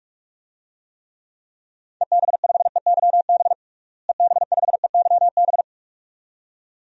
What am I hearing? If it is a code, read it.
Morse code, heard as a keyed tone, about 46 wpm, 700 Hz. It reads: EB5EYB EB5EYB
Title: DXCC-100 Call Signs - 6 Characters - Repeated 2x - 45wpm